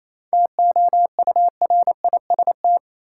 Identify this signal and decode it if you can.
Morse code, heard as a keyed tone, about 27 wpm, 700 Hz. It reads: TOURIST
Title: Seven Letter Words - 28wpm